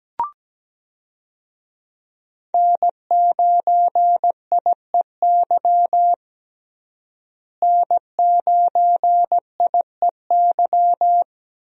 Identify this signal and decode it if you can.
Morse code, heard as a keyed tone, about 17 wpm, 700 Hz. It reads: N9IEY N9IEY
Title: USA Call Signs - 5 Characters - Repeated 2x - 17wpm